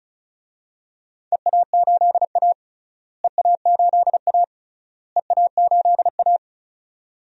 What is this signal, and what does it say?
Morse code, heard as a keyed tone, about 36 wpm, 700 Hz. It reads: EA8A EA8A EA8A
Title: DXCC-100 Call Signs - 4 Characters - Repeated 3x - 35wpm